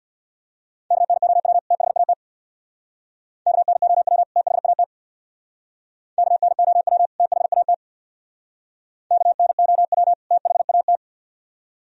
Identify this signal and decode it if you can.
Morse code, heard as a keyed tone, about 66 wpm, 700 Hz. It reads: KNOW THAT KNOW THAT KNOW THAT KNOW THAT